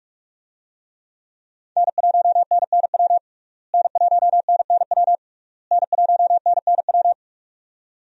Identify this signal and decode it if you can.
Morse code, heard as a keyed tone, about 46 wpm, 700 Hz. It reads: N1NNW N1NNW N1NNW
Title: USA Call Signs - 5 Characters - Repeated 3x - 45wpm